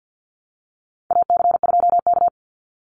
Morse code, heard as a keyed tone, about 50 wpm, 700 Hz. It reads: AC2K
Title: Call Signs Repeated 2x - 50wpm